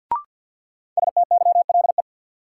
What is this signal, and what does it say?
Morse code, heard as a keyed tone, about 46 wpm, 700 Hz. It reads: STYLE
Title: Sentences from Top 800 Words - New Word Drill - 50wpm